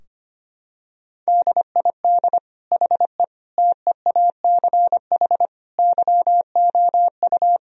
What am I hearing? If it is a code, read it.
Morse code, heard as a keyed tone, about 25 wpm, 700 Hz. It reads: DID HE TEACH YOU
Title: Sentences from Top 1000 Words - Review Part 2 - 25wpm